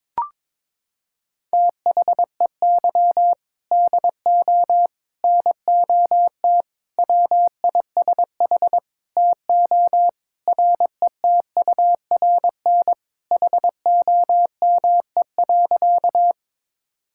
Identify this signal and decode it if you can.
Morse code, heard as a keyed tone, about 22 wpm, 700 Hz. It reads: THEY DO NOT WISH TO RETURN HOME.
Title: Sentences from Top 1600 Words 22wpm